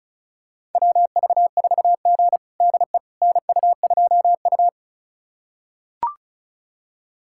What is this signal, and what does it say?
Morse code, heard as a keyed tone, about 36 wpm, 700 Hz. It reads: WV4G DE NU2U
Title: USA Callsign DE Callsign - 4 Characters - 35wpm